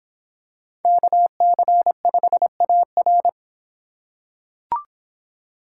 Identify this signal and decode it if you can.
Morse code, heard as a keyed tone, about 26 wpm, 700 Hz. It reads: KC5AR